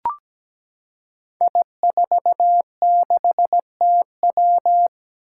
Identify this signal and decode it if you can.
Morse code, heard as a keyed tone, about 17 wpm, 700 Hz. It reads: I46TW
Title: Five Challenging Letter-Number Combinations 17wpm